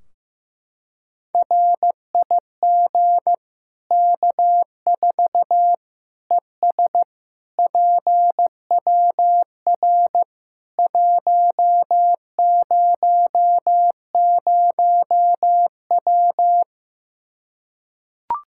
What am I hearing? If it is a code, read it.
Morse code, heard as a keyed tone, about 15 wpm, 700 Hz. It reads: RIG K4 ES PWR 100W
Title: K1USN SST QSO Practice 15wpm